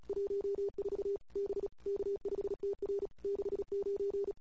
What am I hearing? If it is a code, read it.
Morse code, heard as a keyed tone, about 36 wpm, 410 Hz. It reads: 14 B K5TR 69
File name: CW / 2008-ss-cw-k5tr-op-n5rz -21:00